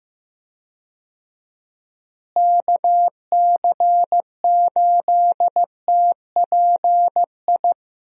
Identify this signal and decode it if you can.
Morse code, heard as a keyed tone, about 15 wpm, 700 Hz. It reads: KC8TPI
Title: USA Call Signs - 6 Characters - Repeated 2x - 15wpm